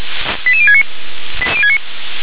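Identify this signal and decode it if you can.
Five-tone selective call (ZVEI), heard as quick runs of five tones, tones of about 70 ms. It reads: 88069
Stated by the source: ZVEI five-tone call received on SDR, 357 channel.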